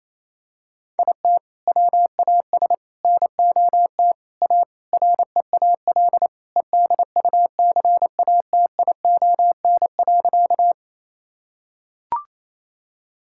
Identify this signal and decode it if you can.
Morse code, heard as a keyed tone, about 28 wpm, 700 Hz. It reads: IT WAS NOT A REAL EDUCATION.
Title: Sentences from Top 1000 Words - Review Part 6 - 28wpm